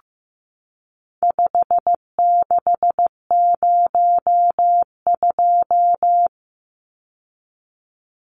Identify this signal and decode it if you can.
Morse code, heard as a keyed tone, about 15 wpm, 700 Hz. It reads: 5602
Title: Numbers - 4-Digit 15wpm